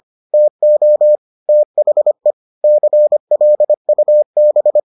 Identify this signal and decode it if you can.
Morse code, heard as a keyed tone, about 25 wpm, 600 Hz. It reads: TO THE CLUB